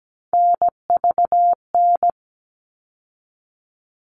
Morse code, heard as a keyed tone, about 17 wpm, 700 Hz. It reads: NVN
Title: All Cut Numbers - 3-Digit 17wpm